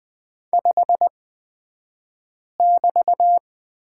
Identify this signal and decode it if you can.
Morse code, heard as a keyed tone, about 20 wpm, spaced as for 5 wpm, 700 Hz. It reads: 5<BT>
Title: 223 Lesson 36d - BT - Two Characters 20wpm